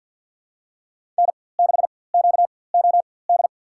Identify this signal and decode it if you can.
Morse code, heard as a keyed tone, about 46 wpm, spaced as for 23 wpm, 700 Hz. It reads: N6QOD
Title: USA Call Signs - 5 Characters - 4x Character Spacing - 50wpm